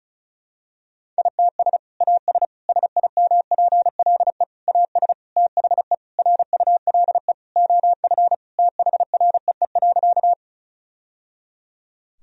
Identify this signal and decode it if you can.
Morse code, heard as a keyed tone, about 34 wpm, 700 Hz. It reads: ITS AS SIMPLE AS THE RULE OF THREE.
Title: Sentences from Top 1000 Words - Review Part 2 - 35wpm